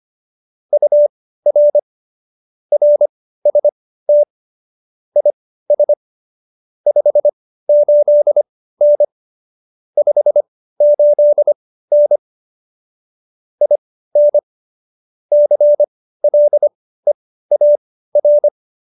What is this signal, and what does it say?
Morse code, heard as a keyed tone, about 25 wpm, spaced as for 15 wpm, 590 Hz. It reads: UR RST IS 58N 58N IN CLEAR
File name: qso222 15